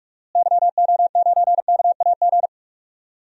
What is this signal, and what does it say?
Morse code, heard as a keyed tone, about 41 wpm, 700 Hz. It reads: YO9KAG
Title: DXCC-100 Call Signs - 6 Characters - Repeated 3x - 45wpm